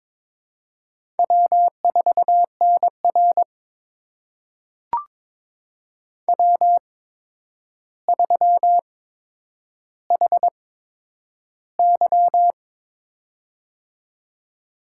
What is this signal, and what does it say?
Morse code, heard as a keyed tone, about 22 wpm, spaced as for 9 wpm, 700 Hz. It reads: W4NR W3HY
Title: USA Call Signs - 4 Characters - 8x Character Spacing - 22wpm